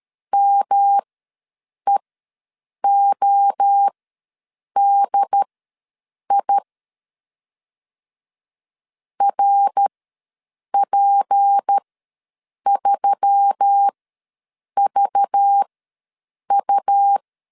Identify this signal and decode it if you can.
Morse code, heard as a keyed tone, about 13 wpm, spaced as for 7 wpm, 800 Hz. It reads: MEODI RP3VU
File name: LEZIONI / 7 GF9Z / 7 GF9 ws 12wpm 3344